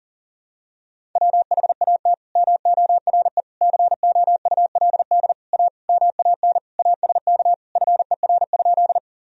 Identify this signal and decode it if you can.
Morse code, heard as a keyed tone, about 40 wpm, 700 Hz. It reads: WHAT MORE COULD A MAN ASK FER?
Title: Sentences from Top 200 Words and Abbreviations 40wpm